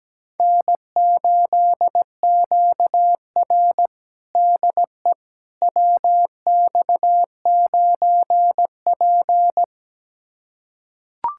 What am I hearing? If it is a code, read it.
Morse code, heard as a keyed tone, about 17 wpm, 700 Hz. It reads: N8QR DE WX9P
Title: USA Callsign DE Callsign - 4 Characters - 17wpm